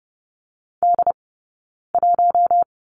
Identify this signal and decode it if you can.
Morse code, heard as a keyed tone, about 30 wpm, 700 Hz. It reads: D 1
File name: Lesson 19d - Letter w - Two Characters 30wpm